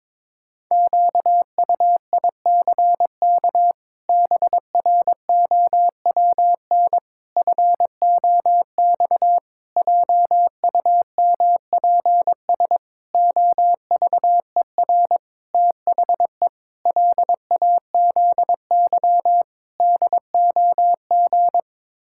Morse code, heard as a keyed tone, about 22 wpm, 700 Hz. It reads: QUICK BROWN FOX JUMPS OVER THE LAZY DOG
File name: practice / 22wpm / Sets of 8 Words - 22wpm